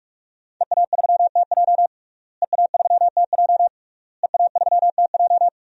Morse code, heard as a keyed tone, about 48 wpm, 700 Hz. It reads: EA3TJ EA3TJ EA3TJ
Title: DXCC-100 Call Signs - 5 Characters - Repeated 3x - 45wpm